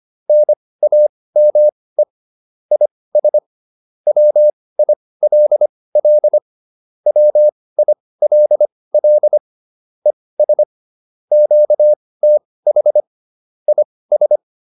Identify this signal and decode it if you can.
Morse code, heard as a keyed tone, about 25 wpm, spaced as for 18 wpm, 600 Hz. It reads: NAME IS WILL WILL ES QTH IS
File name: QSO409 18